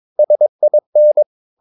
Morse code, heard as a keyed tone, about 22 wpm, 600 Hz. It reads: SIN
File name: Ferie QRS 22wpm